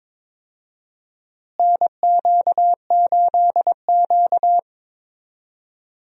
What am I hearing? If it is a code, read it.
Morse code, heard as a keyed tone, about 22 wpm, 700 Hz. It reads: NQ8Q
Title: USA Call Signs - 4 Characters - Repeated 2x - 22wpm